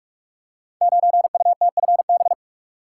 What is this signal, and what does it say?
Morse code, heard as a keyed tone, about 44 wpm, 700 Hz. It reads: 9UTFB
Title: Code Groups - 5 Letter:Numbers - 45wpm